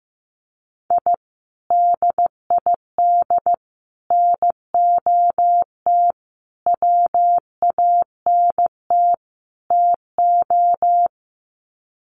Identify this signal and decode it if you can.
Morse code, heard as a keyed tone, about 15 wpm, 700 Hz. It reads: I DID NOT WANT TO
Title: Sets of 5 Words 15wpm